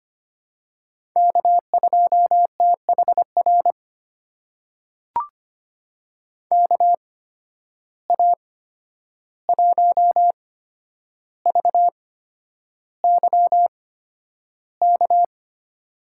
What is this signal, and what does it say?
Morse code, heard as a keyed tone, about 25 wpm, spaced as for 10 wpm, 700 Hz. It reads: K2THR KA1VYK
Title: USA Call Signs - 8x Character Spacing - 25wpm